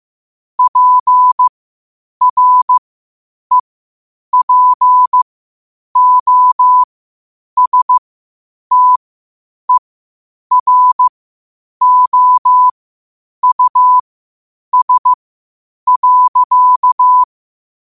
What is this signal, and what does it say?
Morse code, heard as a keyed tone, about 15 wpm, spaced as for 8 wpm, 1000 Hz. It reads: PREPOSTEROUS.
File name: qotdfiles / 8wpm